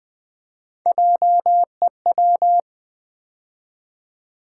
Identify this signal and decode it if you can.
Morse code, heard as a keyed tone, about 20 wpm, 700 Hz. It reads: JEW
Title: Build A Word - 3 Letters At A Time - 9-Letter Words - 20wpm